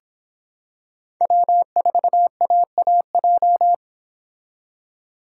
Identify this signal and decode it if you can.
Morse code, heard as a keyed tone, about 26 wpm, 700 Hz. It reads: W4AAJ